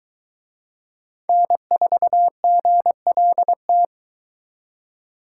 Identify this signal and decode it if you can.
Morse code, heard as a keyed tone, about 23 wpm, 700 Hz. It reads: N4GLT